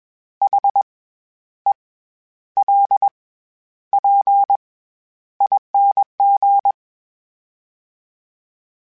Morse code, heard as a keyed tone, about 21 wpm, 800 Hz. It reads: H E L P ING
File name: Special INGs-mp3s / Special INGs-mp3s / SlowToFaster ing @20